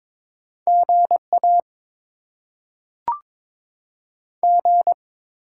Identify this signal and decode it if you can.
Morse code, heard as a keyed tone, about 22 wpm, 700 Hz. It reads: GA G
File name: Top 20 QSO Elements - 6x Character Spacing - 22wpm